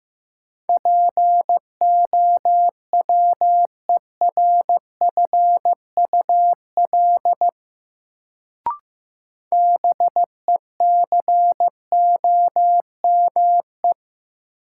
Morse code, heard as a keyed tone, about 15 wpm, 700 Hz. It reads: POWERFUL BECOME
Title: Sets of 5 Words Encore - New Word Drill - 15wpm